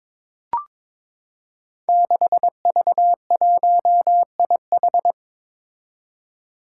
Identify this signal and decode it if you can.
Morse code, heard as a keyed tone, about 22 wpm, 700 Hz. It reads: 6V1IH
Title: Five Challenging Letter-Number Combinations 22wpm